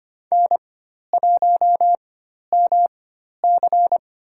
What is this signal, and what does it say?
Morse code, heard as a keyed tone, about 25 wpm, spaced as for 12 wpm, 700 Hz. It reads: N1MC
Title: USA Call Signs - 4x Character Spacing - 25wpm